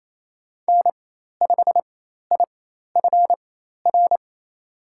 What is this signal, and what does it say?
Morse code, heard as a keyed tone, about 28 wpm, spaced as for 13 wpm, 700 Hz. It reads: N5IFR
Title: USA Call Signs - 5 Characters - 4x Character Spacing - 28wpm